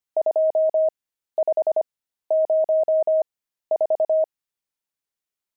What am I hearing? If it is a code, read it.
Morse code, heard as a keyed tone, about 25 wpm, spaced as for 13 wpm, 630 Hz. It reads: 2504
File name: pangram-fs-13-cs-25-nums15